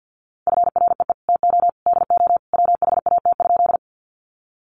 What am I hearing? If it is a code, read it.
Morse code, heard as a keyed tone, about 50 wpm, 700 Hz. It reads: FREE TO DO WHAT?
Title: Sentences from Top 500 Words 50wpm